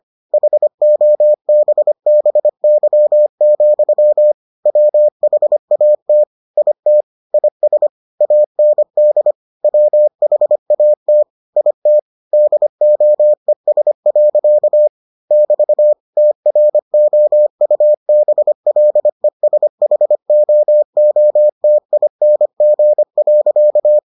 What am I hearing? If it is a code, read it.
Morse code, heard as a keyed tone, about 25 wpm, 600 Hz. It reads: HOBBY, WHAT IT IS AND WHAT IT DOES. <BT> TROUBLESHOOTING.